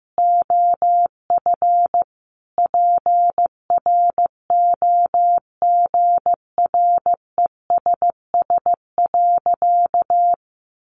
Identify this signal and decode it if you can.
Morse code, heard as a keyed tone, about 15 wpm, 700 Hz. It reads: OF PROGRESS.
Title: Sentences from Top 1400 Words 15wpm